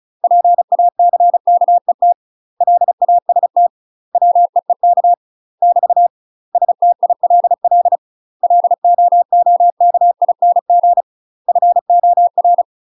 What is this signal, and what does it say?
Morse code, heard as a keyed tone, about 35 wpm, 700 Hz. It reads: PACKET LAST WEEK <BT> STILL LOOKING FOR
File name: UK MASTER QSOs TRAINING / 35WPM / UK MASTER QSOs 0005